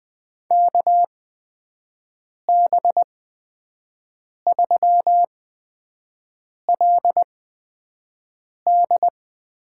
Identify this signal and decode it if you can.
Morse code, heard as a keyed tone, about 20 wpm, spaced as for 5 wpm, 700 Hz. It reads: KB3LD
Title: USA Call Signs - 6 Characters - 8x Character Spacing - 20wpm